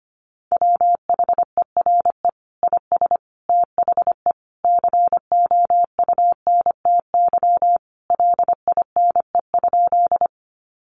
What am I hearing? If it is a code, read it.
Morse code, heard as a keyed tone, about 25 wpm, 700 Hz. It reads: WHERE IS THE COUNTY LINE?
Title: Sentences from Top 1200 Words 25wpm